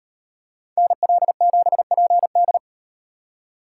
Morse code, heard as a keyed tone, about 38 wpm, 700 Hz. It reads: NL7PD